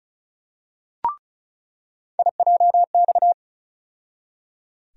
Morse code, heard as a keyed tone, about 36 wpm, 700 Hz. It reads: IJX
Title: USA Call Sign Postfixes - 3 Letters 35wpm